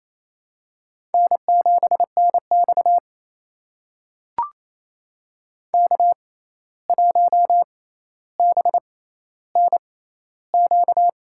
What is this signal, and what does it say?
Morse code, heard as a keyed tone, about 28 wpm, spaced as for 13 wpm, 700 Hz. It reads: N7NX K1BNQ
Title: USA Call Signs - 6x Character Spacing - 28wpm